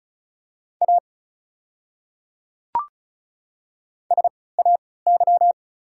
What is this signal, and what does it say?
Morse code, heard as a keyed tone, about 34 wpm, spaced as for 20 wpm, 690 Hz. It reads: A SAY
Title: 3x Character Spacing - Top 300 Words - 35wpm